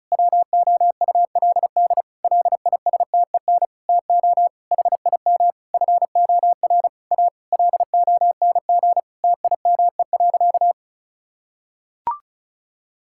Morse code, heard as a keyed tone, about 34 wpm, 700 Hz. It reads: WOULD LISTEN TO HIM FOR A LONG TIME.
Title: Sentences from Top 1600 Words 35wpm